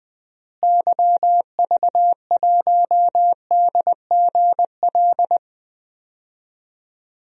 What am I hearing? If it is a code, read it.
Morse code, heard as a keyed tone, about 20 wpm, 700 Hz. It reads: YV1DGL
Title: DXCC-100 Call Signs - 6 Characters - Repeated 3x - 20wpm